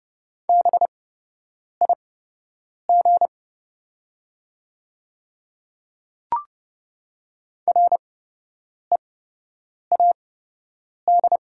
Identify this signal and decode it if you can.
Morse code, heard as a keyed tone, about 30 wpm, spaced as for 8 wpm, 690 Hz. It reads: BIG READ
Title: 8x Character Spacing - Top 200 Words - 30wpm